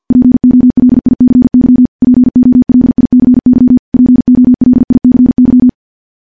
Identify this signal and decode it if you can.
Morse code, heard as a keyed tone, about 50 wpm, 250 Hz. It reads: POLICY POLICY POLICY
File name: morse / morse-training-50WPM